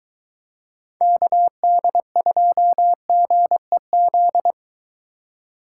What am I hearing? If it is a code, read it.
Morse code, heard as a keyed tone, about 22 wpm, 700 Hz. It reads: KD2GEZ